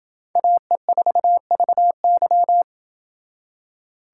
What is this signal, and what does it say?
Morse code, heard as a keyed tone, about 28 wpm, 700 Hz. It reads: AE4VY